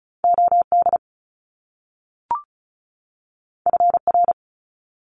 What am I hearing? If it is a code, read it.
Morse code, heard as a keyed tone, about 36 wpm, 700 Hz. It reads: OD FR
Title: Top Two Letter Combinations in Words 35wpm